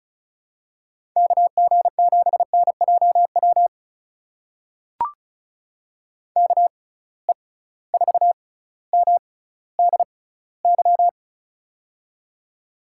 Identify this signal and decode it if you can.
Morse code, heard as a keyed tone, about 36 wpm, spaced as for 18 wpm, 700 Hz. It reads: KG7NJW KE4MDY
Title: USA Call Signs - 6 Characters - 6x Character Spacing - 35wpm